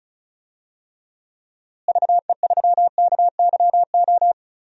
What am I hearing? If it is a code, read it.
Morse code, heard as a keyed tone, about 34 wpm, 700 Hz. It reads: VE3KYO
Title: DXCC-100 Call Signs - 6 Characters - Repeated 2x - 35wpm